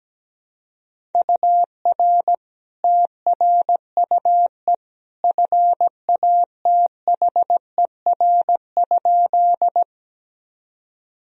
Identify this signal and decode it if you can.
Morse code, heard as a keyed tone, about 17 wpm, 700 Hz. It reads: UR TRUE FATHER?
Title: Sentences from Top 400 Words and Abbreviations 17wpm